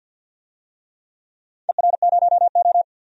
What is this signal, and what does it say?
Morse code, heard as a keyed tone, about 51 wpm, 700 Hz. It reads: ER0O
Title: DXCC-100 Call Signs - 4 Characters - 50wpm